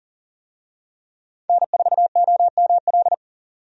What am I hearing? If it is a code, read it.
Morse code, heard as a keyed tone, about 39 wpm, 700 Hz. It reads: N4OML
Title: USA Call Signs - 5 Characters - 40wpm